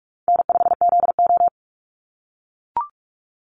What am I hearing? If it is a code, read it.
Morse code, heard as a keyed tone, about 44 wpm, 700 Hz. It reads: N5ZO
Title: Call Signs - 45wpm